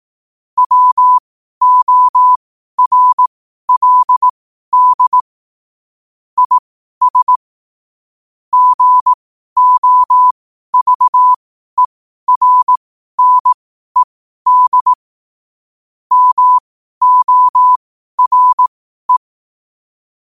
Morse code, heard as a keyed tone, about 18 wpm, spaced as for 12 wpm, 1000 Hz. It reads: WORLD IS GOVERNED MORE
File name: qotdfiles / 12wpm